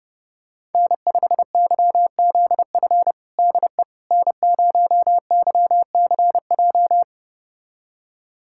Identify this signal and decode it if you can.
Morse code, heard as a keyed tone, about 30 wpm, 700 Hz. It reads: N5YZF DE N0YCJ
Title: USA Callsign DE Callsign - 5 Characters - 30wpm